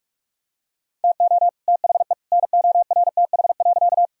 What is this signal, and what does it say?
Morse code, heard as a keyed tone, about 44 wpm, 700 Hz. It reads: TO THE NORTH.
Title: Sentences from Top 1000 Words - Review Part 1 - 45wpm